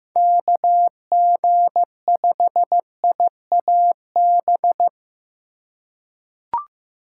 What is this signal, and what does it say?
Morse code, heard as a keyed tone, about 15 wpm, 700 Hz. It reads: KG5IAB
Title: USA Call Signs - 6 Characters - 6x Character Spacing - 15wpm